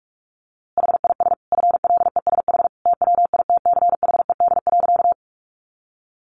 Morse code, heard as a keyed tone, about 44 wpm, 700 Hz. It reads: HIS FLESH TWITCHED.
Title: A Sound Of Thunder 45wpm